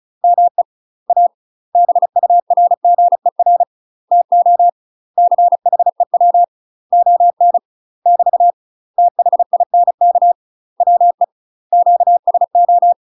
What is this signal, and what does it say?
Morse code, heard as a keyed tone, about 34 wpm, 700 Hz. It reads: ME A BURGER TO CHEW ON <BT> THINK WE QSO